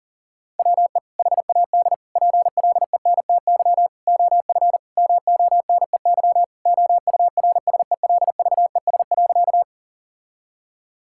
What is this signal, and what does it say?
Morse code, heard as a keyed tone, about 40 wpm, 700 Hz. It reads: WE HAD PLENTY OF MONEY OURSELVES.
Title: Sentences from Top 1600 Words 40wpm